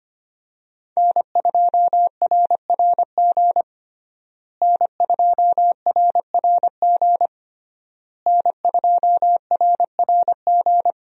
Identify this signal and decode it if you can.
Morse code, heard as a keyed tone, about 25 wpm, 700 Hz. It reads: N2RRG N2RRG N2RRG
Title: USA Call Signs - 5 Characters - Repeated 3x - 25wpm